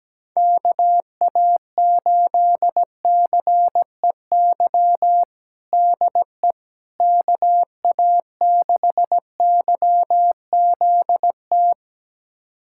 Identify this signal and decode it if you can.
Morse code, heard as a keyed tone, about 17 wpm, 700 Hz. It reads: KA8CEY DE KA6YZT
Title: USA Callsign DE Callsign - 6 Characters - 17wpm